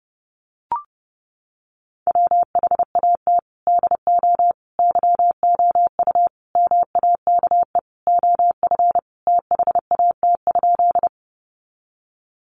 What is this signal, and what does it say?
Morse code, heard as a keyed tone, about 30 wpm, 700 Hz. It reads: WHAT DO YOU MAKE OF THAT?
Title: Sentences from Top 100 Words 30wpm